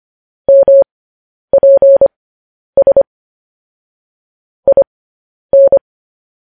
Morse code, heard as a keyed tone, about 25 wpm, spaced as for 10 wpm, 550 Hz. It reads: MPS IN
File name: podcast / News headlines-1x-25x10